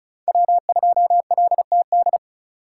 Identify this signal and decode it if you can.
Morse code, heard as a keyed tone, about 36 wpm, 700 Hz. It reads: W2LTD
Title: USA Call Signs - 5 Characters - 2x Character Spacing - 35wpm